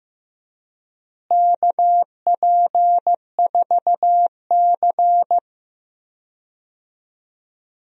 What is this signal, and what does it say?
Morse code, heard as a keyed tone, about 15 wpm, 700 Hz. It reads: KP4C
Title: DXCC-100 Call Signs - 4 Characters - Repeated 3x - 15wpm